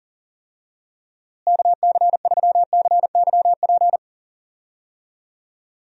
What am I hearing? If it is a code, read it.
Morse code, heard as a keyed tone, about 37 wpm, 700 Hz. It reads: KC3CYP
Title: USA Call Signs - 6 Characters - Repeated 3x - 40wpm